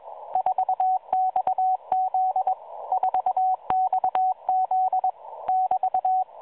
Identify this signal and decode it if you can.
Morse code, heard as a keyed tone, about 21 wpm, 750 Hz, with radio noise. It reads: 4XZ 4XZ <BT>